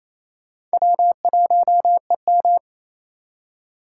Morse code, heard as a keyed tone, about 36 wpm, 700 Hz. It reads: W1EM